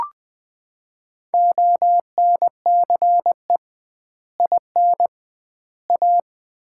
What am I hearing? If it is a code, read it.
Morse code, heard as a keyed tone, about 20 wpm, 700 Hz. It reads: ONCE IN A
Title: 2x Word-Spacing - Sets of 4 Words - 20wpm